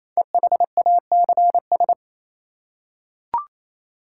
Morse code, heard as a keyed tone, about 27 wpm, 700 Hz. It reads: EHACS
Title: Code Groups - 5 Letters - 28wpm